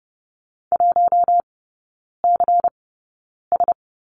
Morse code, heard as a keyed tone, about 30 wpm, spaced as for 9 wpm, 700 Hz. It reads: 1CS
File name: Lesson 19e - Letter w - Three Characters 30wpm